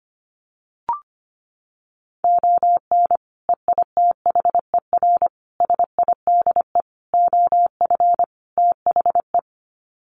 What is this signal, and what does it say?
Morse code, heard as a keyed tone, about 25 wpm, 700 Hz. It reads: ON EITHER SIDE OF THE
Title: Sets of 5 Words 25wpm